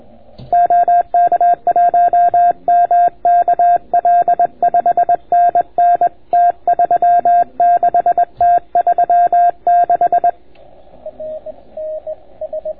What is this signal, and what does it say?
Morse code, heard as a keyed tone, about 21 wpm, 700 Hz. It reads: OK1MKL 5NN T36 T36